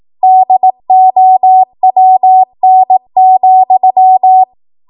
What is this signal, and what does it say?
Morse code, heard as a keyed tone, about 18 wpm, 750 Hz. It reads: DOWN,
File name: Archive / 18 WPM / 18WPM